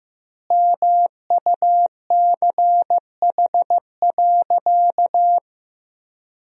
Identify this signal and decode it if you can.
Morse code, heard as a keyed tone, about 15 wpm, 700 Hz. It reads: MUCH.
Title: 2x Word-Spacing - Sentences from Top 500 Words - 15wpm